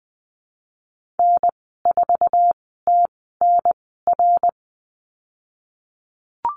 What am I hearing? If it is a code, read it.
Morse code, heard as a keyed tone, about 20 wpm, spaced as for 15 wpm, 700 Hz. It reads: N4TNR
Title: 088 Lesson 15g - Number 5 - Call Signs 20wpm